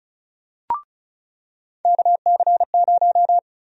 Morse code, heard as a keyed tone, about 34 wpm, 700 Hz. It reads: KC0
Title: USA Call Sign Prefixes - 35wpm